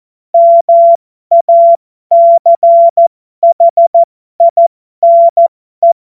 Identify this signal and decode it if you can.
Morse code, heard as a keyed tone, about 14 wpm, spaced as for 12 wpm, 680 Hz. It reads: MACHINE